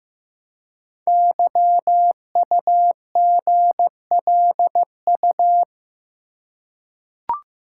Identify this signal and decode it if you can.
Morse code, heard as a keyed tone, about 15 wpm, 700 Hz. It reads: YUGLU
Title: Code Groups - 5 Letter:Numbers - 15wpm